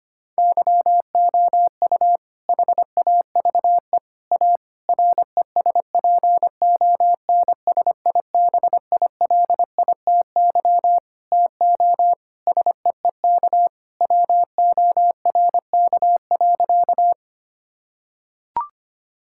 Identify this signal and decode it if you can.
Morse code, heard as a keyed tone, about 25 wpm, 700 Hz. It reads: YOU HAVE A RESPONSIBILITY TO SEEK WORK.
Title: Sentences from Top 1600 Words 25wpm